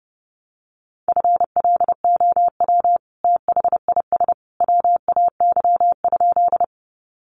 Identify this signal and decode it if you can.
Morse code, heard as a keyed tone, about 30 wpm, 700 Hz. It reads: FLOW THIS WAY?
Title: Sentences from Top 1200 Words 30wpm